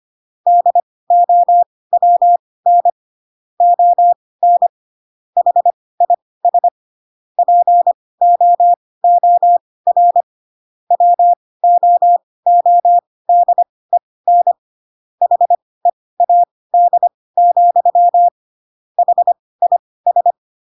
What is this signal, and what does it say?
Morse code, heard as a keyed tone, about 25 wpm, spaced as for 18 wpm, 700 Hz. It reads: DOWN ON HIS POOR WOODEN HEAD, HIS
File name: Pinocchio / Pinocchio ch6 12 25 ik1hge